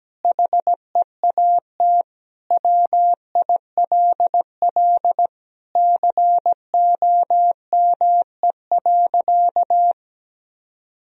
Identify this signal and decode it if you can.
Morse code, heard as a keyed tone, about 17 wpm, 700 Hz. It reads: HEAT WILL COME.
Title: Sentences from Top 1000 Words - Review Part 3 - 17wpm